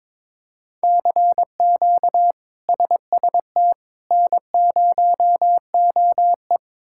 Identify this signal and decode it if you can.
Morse code, heard as a keyed tone, about 21 wpm, 700 Hz. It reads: CQ SST N0OE
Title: K1USN SST QSO Practice - 22wpm